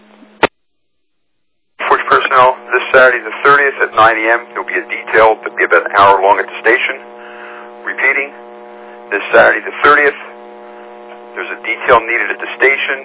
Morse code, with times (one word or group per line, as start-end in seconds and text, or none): none